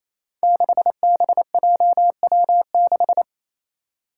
Morse code, heard as a keyed tone, about 28 wpm, 700 Hz. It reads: 6BJW6
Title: Five Challenging Letter-Number Combinations 28wpm